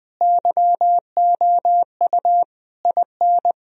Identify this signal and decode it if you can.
Morse code, heard as a keyed tone, about 20 wpm, 700 Hz. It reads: YOU IN
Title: Sentences from Top 1000 Words - Review Part 1 - 20wpm